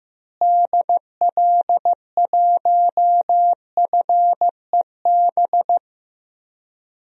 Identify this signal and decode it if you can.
Morse code, heard as a keyed tone, about 15 wpm, 700 Hz. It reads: DL1FEB
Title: DXCC-100 Call Signs - 6 Characters - Repeated 3x - 15wpm